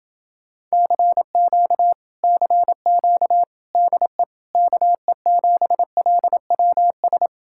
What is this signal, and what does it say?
Morse code, heard as a keyed tone, about 29 wpm, 700 Hz. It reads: CQ CQ DE KE7LWS